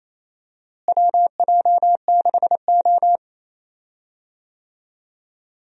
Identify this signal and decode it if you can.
Morse code, heard as a keyed tone, about 28 wpm, 700 Hz. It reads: WJ6O
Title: USA Call Signs - 4 Characters - Repeated 2x - 28wpm